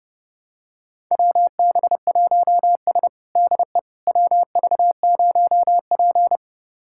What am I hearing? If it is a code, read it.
Morse code, heard as a keyed tone, about 32 wpm, 700 Hz. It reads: WB1S DE WV0P
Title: USA Callsign DE Callsign - 4 Characters - 30wpm